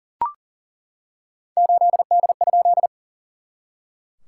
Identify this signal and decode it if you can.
Morse code, heard as a keyed tone, about 40 wpm, 700 Hz. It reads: 8D?
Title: SOTA QSO Practice 40wpm